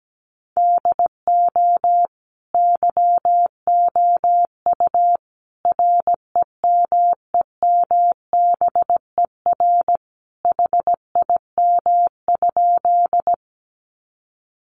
Morse code, heard as a keyed tone, about 17 wpm, 700 Hz. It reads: DO YOU REMEMBER HIM?
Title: Sentences from Top 400 Words 17wpm